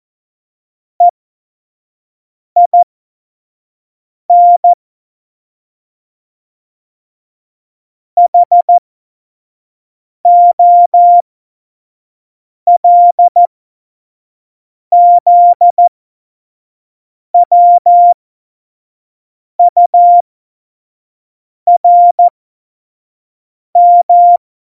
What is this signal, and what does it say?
Morse code, heard as a keyed tone, about 14 wpm, spaced as for 5 wpm, 700 Hz. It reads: EIN HOLZWURM
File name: Morsekurs / Witze / Witz 18. September 2018, 25 F70